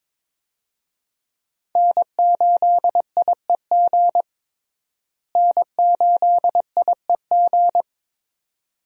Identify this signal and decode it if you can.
Morse code, heard as a keyed tone, about 22 wpm, 700 Hz. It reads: N8IEG N8IEG
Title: USA Call Signs - 5 Characters - Repeated 2x - 22wpm